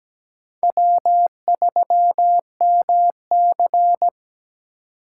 Morse code, heard as a keyed tone, about 20 wpm, 700 Hz. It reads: W3MC
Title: Speed Racing - USA 1x2 Call Signs 17wpm